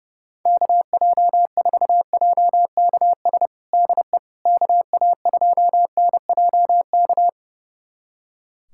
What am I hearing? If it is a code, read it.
Morse code, heard as a keyed tone, about 32 wpm, 700 Hz. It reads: KJ4JKS DE KA2NJK
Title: USA Callsign DE Callsign - 6 Characters - 30wpm